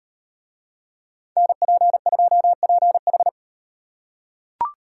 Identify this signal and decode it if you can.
Morse code, heard as a keyed tone, about 38 wpm, 700 Hz. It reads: NP2PH